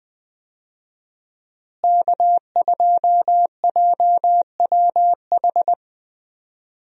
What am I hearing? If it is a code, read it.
Morse code, heard as a keyed tone, about 20 wpm, 700 Hz. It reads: K2JWH
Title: USA Call Signs - 5 Characters - Repeated 2x - 20wpm